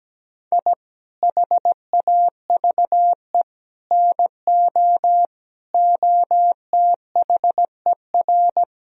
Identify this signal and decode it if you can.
Morse code, heard as a keyed tone, about 17 wpm, 700 Hz. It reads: I HAVE NO OTHER
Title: Sentences from Top 1000 Words - Review Part 3 - 17wpm